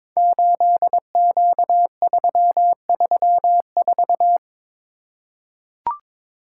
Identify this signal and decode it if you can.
Morse code, heard as a keyed tone, about 22 wpm, 700 Hz. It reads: 8Q334
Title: Code Groups - 5 Letter:Numbers - 22wpm